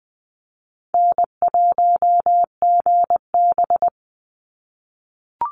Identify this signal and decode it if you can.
Morse code, heard as a keyed tone, about 20 wpm, 700 Hz. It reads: N1GB
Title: NA SOTA Chaser Callsigns 20wpm